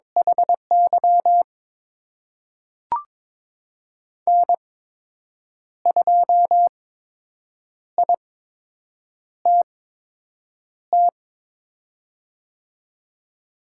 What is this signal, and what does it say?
Morse code, heard as a keyed tone, about 22 wpm, spaced as for 7 wpm, 700 Hz. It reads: HY N2ITT
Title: USA Call Signs - 8x Character Spacing - 22wpm